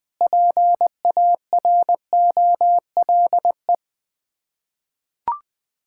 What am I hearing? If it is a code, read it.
Morse code, heard as a keyed tone, about 20 wpm, 700 Hz. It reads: PAROLE
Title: Sets of 8 Words - 20wpm